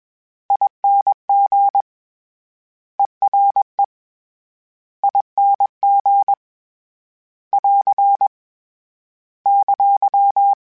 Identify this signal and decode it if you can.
Morse code, heard as a keyed tone, about 21 wpm, 800 Hz. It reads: ING ERE ING <AR> !
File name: Special INGs / Special INGs-mp3s / Ere ing @20